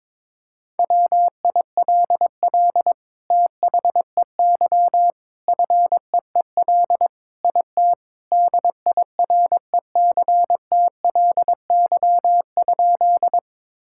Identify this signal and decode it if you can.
Morse code, heard as a keyed tone, about 22 wpm, 700 Hz. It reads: WILL THEY FEEL IT DIRECTLY?
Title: Sentences from Top 1000 Words - Review Part 5 - 22wpm